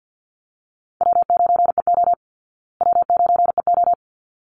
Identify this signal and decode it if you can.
Morse code, heard as a keyed tone, about 50 wpm, 700 Hz. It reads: W9EO W9EO
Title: Call Signs Repeated 3x - 50wpm